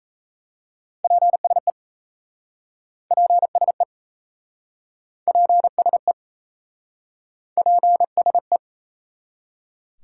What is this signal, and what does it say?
Morse code, heard as a keyed tone, about 38 wpm, 700 Hz. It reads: PSE PSE PSE PSE